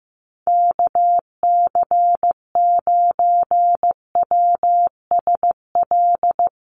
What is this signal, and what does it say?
Morse code, heard as a keyed tone, about 15 wpm, 700 Hz. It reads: KC9WSL
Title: USA Call Signs Repeated 2x 15wpm